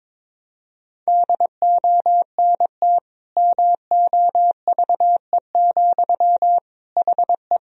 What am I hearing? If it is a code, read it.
Morse code, heard as a keyed tone, about 22 wpm, 700 Hz. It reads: DONT MOVE, HE
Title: Sentences from Top 1000 Words - Review Part 1 - 22wpm